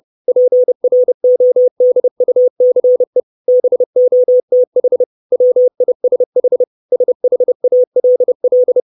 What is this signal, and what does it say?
Morse code, heard as a keyed tone, about 30 wpm, 500 Hz. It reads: PRODUCE BOTH WISH SHALL
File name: 632 Words 30WPM